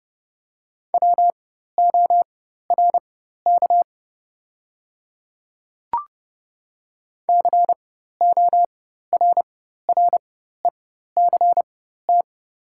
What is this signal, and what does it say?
Morse code, heard as a keyed tone, about 30 wpm, spaced as for 14 wpm, 690 Hz. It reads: WORK CORRECT
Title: 4x Character Spacing - Top 500 Words - 30wpm